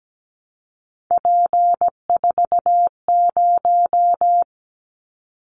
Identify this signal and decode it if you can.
Morse code, heard as a keyed tone, about 17 wpm, 700 Hz. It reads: P40
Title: Callsign Prefixes for Top 100 DXCC - 17wpm